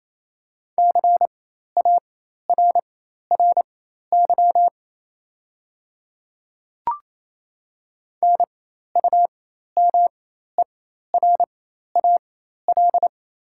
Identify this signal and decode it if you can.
Morse code, heard as a keyed tone, about 27 wpm, spaced as for 13 wpm, 700 Hz. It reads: CARRY NUMERAL
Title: 4x Character Spacing - Top 500 Words - 28wpm